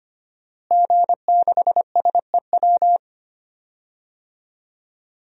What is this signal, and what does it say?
Morse code, heard as a keyed tone, about 25 wpm, 700 Hz. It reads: G6SEW
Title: DXCC-100 Call Signs - 5 Characters - Repeated 2x - 25wpm